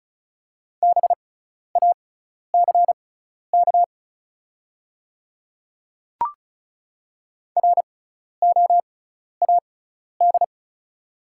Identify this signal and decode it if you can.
Morse code, heard as a keyed tone, about 36 wpm, spaced as for 12 wpm, 700 Hz. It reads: BACK ROAD
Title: 6x Character Spacing - Top 500 Words - 35wpm